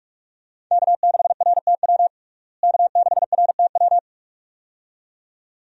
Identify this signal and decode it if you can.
Morse code, heard as a keyed tone, about 43 wpm, 700 Hz. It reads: K6RTW K6RTW
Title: USA Call Signs - 5 Characters - Repeated 2x - 45wpm